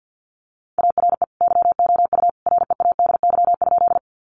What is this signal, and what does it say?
Morse code, heard as a keyed tone, about 48 wpm, 700 Hz. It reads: ARE YOU READY?
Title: Sentences from Top 400 Words 50wpm